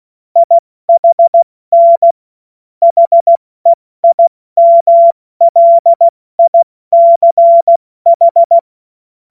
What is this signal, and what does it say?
Morse code, heard as a keyed tone, about 16 wpm, 680 Hz. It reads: IHN HEIMLICH
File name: Morsekurs / Witze / Witz 03. Januar 2023 70 F80